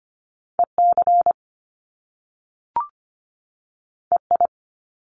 Morse code, heard as a keyed tone, about 25 wpm, 700 Hz. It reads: EC EI
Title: Top Two Letter Combinations in Words 25wpm